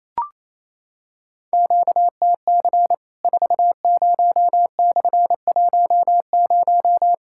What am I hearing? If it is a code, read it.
Morse code, heard as a keyed tone, about 28 wpm, 700 Hz. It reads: QTC 40/10
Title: Worked All Europe DX Contest - QTC Exchange - 28wpm